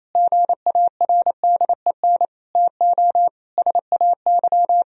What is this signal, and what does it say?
Morse code, heard as a keyed tone, about 28 wpm, 700 Hz. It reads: GARDEN TO SAY
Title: Sentences from Top 1000 Words - Review Part 4 - 28wpm